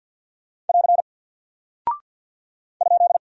Morse code, heard as a keyed tone, about 49 wpm, 700 Hz. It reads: <AR> ?
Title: Single Character (Letter:Number:Punctuation:Prosign) Alt 50wpm